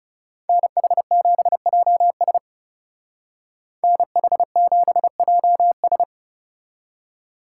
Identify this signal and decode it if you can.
Morse code, heard as a keyed tone, about 36 wpm, 700 Hz. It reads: NH7JS NH7JS
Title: Speed Racing - USA 2x2 Call Signs 30wpm